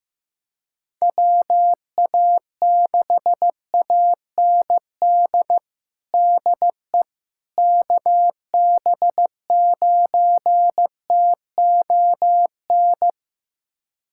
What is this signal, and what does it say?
Morse code, heard as a keyed tone, about 15 wpm, 700 Hz. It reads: WA6AND DE KB9TON
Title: USA Callsign DE Callsign 15wpm